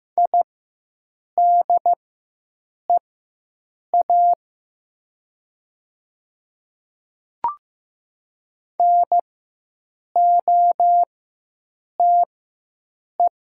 Morse code, heard as a keyed tone, about 15 wpm, spaced as for 7 wpm, 700 Hz. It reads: IDEA NOTE
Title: 4x Character Spacing - Top 500 Words - 15wpm